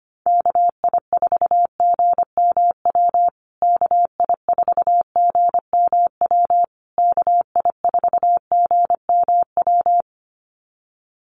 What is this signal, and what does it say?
Morse code, heard as a keyed tone, about 25 wpm, 700 Hz. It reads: KI4GMW KI4GMW KI4GMW
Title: USA Call Signs Repeated 3x 25wpm